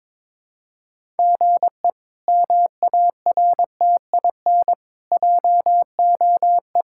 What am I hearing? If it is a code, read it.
Morse code, heard as a keyed tone, about 23 wpm, 700 Hz. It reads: GE MARTIN JOE
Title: K1USN SST QSO Practice - 22wpm